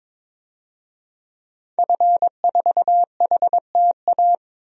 Morse code, heard as a keyed tone, about 22 wpm, 700 Hz. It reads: F4HTA
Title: DXCC-100 Call Signs - 5 Characters - 22wpm